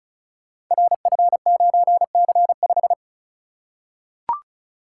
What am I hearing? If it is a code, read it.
Morse code, heard as a keyed tone, about 36 wpm, 700 Hz. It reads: RF9C5
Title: Code Groups - 5 Letter:Numbers - 35wpm